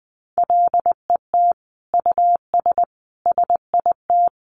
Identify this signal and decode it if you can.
Morse code, heard as a keyed tone, about 20 wpm, 700 Hz. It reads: LET US SIT
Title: Sentences from Top 500 Words 20wpm